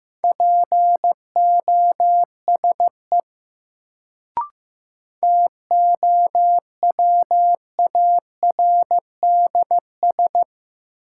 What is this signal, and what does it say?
Morse code, heard as a keyed tone, about 15 wpm, 700 Hz. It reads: POSE TOWARDS
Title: Sentences from Top 1000 Words - Review Part 2 - New Word Drill - 15wpm